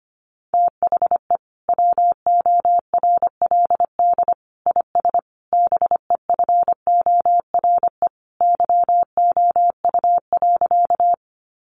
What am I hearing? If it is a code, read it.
Morse code, heard as a keyed tone, about 25 wpm, 700 Hz. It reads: THE WORLD IS BEFORE YOU.
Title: Sentences from Top 200 Words 25wpm